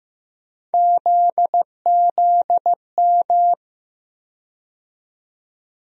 Morse code, heard as a keyed tone, about 15 wpm, 700 Hz. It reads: ZZM
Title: USA Call Sign Postfixes - 3 Letters 15wpm